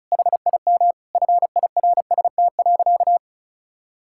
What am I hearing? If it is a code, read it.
Morse code, heard as a keyed tone, about 36 wpm, 700 Hz. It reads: HIM FIRST.
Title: Sentences from Top 1000 Words - Review Part 6 - 35wpm